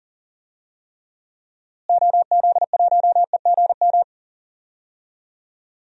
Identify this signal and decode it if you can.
Morse code, heard as a keyed tone, about 38 wpm, 700 Hz. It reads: OZ1EGM
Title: DXCC-100 Call Signs - 6 Characters - Repeated 3x - 40wpm